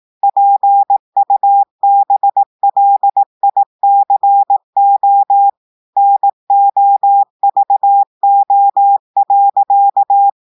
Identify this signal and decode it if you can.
Morse code, heard as a keyed tone, about 18 wpm, 800 Hz. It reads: PUBLICO NOVO.
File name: audios / portugais / portugais texte en clair 2 vitesse 18 mots minutes